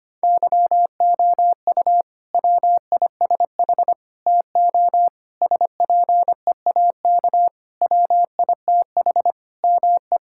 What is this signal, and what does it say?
Morse code, heard as a keyed tone, about 25 wpm, 700 Hz. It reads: YOU WISH TO SPEAK WITH ME
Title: Sentences from Top 1600 Words 25wpm